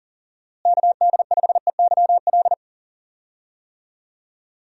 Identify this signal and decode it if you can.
Morse code, heard as a keyed tone, about 38 wpm, 700 Hz. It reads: KD5EYL
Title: USA Call Signs - 6 Characters - Repeated 3x - 40wpm